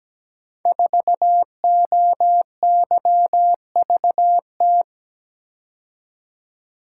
Morse code, heard as a keyed tone, about 17 wpm, 700 Hz. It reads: 4OYVT
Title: Code Groups - 5 Letter:Numbers - 17wpm